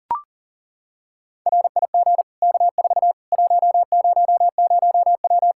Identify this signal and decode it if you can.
Morse code, heard as a keyed tone, about 40 wpm, 700 Hz. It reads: RIG K4 100W
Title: Practice QSO - Exchange 2 - Info - 40wpm